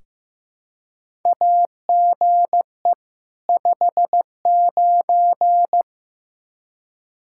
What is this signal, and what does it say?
Morse code, heard as a keyed tone, about 16 wpm, 700 Hz. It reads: AGE 59
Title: Practice QSO - Exchange 3 - More Info - 15wpm